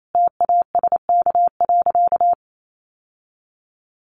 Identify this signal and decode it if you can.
Morse code, heard as a keyed tone, about 28 wpm, 700 Hz. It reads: TASK.
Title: Sentences from Top 1100 Words 28wpm